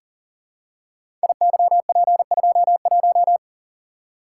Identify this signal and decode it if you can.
Morse code, heard as a keyed tone, about 42 wpm, 700 Hz. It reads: IYP21
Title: Code Groups - 5 Letter:Numbers - 40wpm